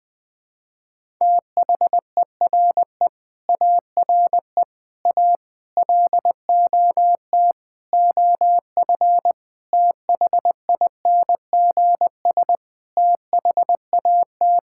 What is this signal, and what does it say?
Morse code, heard as a keyed tone, about 20 wpm, 700 Hz. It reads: THERE ARE A LOT OF THINGS THAT
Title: Sentences with 5-Word Phrases - 20wpm